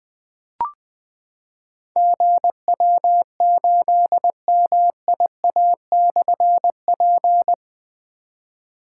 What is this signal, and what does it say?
Morse code, heard as a keyed tone, about 20 wpm, 700 Hz. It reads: GW8MIA/P
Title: Difficult Call Signs 20wpm